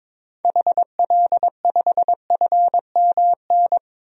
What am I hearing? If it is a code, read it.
Morse code, heard as a keyed tone, about 22 wpm, 700 Hz. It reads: HL5FMN
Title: DXCC-100 Call Signs - 6 Characters - 22wpm